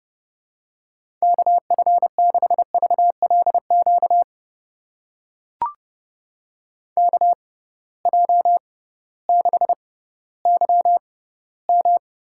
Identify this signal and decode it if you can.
Morse code, heard as a keyed tone, about 30 wpm, spaced as for 16 wpm, 700 Hz. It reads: KF6VLQ KJ6YM
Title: USA Call Signs - 6 Characters - 6x Character Spacing - 30wpm